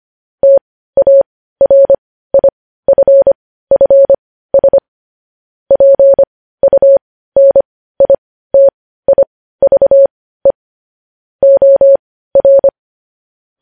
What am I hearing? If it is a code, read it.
Morse code, heard as a keyed tone, about 25 wpm, spaced as for 15 wpm, 550 Hz. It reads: TARIFFS PUNITIVE OR
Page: news headlines in morse code at 15 wpm